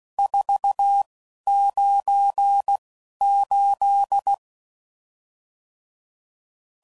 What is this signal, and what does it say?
Morse code, heard as a keyed tone, about 16 wpm, spaced as for 11 wpm, 800 Hz. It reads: 498
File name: CW Code Groups @10WPM / RSTs / Various RST No's